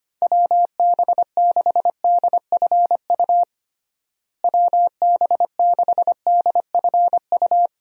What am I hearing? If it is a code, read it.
Morse code, heard as a keyed tone, about 25 wpm, 700 Hz. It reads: WB6DFU WB6DFU
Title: USA Call Signs - 6 Characters - Repeated 2x - 25wpm